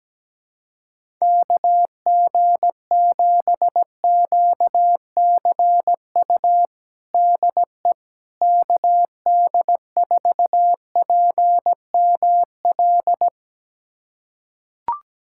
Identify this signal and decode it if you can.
Morse code, heard as a keyed tone, about 17 wpm, 700 Hz. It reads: KG7QCU DE KD4PML
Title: USA Callsign DE Callsign - 6 Characters - 17wpm